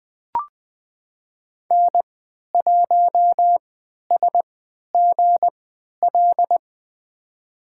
Morse code, heard as a keyed tone, about 20 wpm, spaced as for 11 wpm, 700 Hz. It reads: N1SGL
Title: USA Call Signs - 5 Characters - 3x Character Spacing - 20wpm